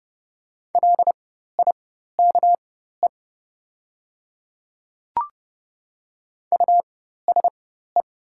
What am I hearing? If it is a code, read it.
Morse code, heard as a keyed tone, about 30 wpm, spaced as for 14 wpm, 700 Hz. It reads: LIKE USE
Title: 4x Character Spacing - Top 200 Words - 30wpm